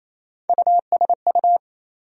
Morse code, heard as a keyed tone, about 28 wpm, 700 Hz. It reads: USU
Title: USA Call Sign Postfixes - 3 Letters 28wpm